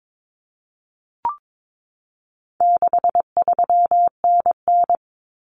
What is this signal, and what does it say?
Morse code, heard as a keyed tone, about 22 wpm, 700 Hz. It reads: 63NN
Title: Common Cut Numbers - 4-Digit 22wpm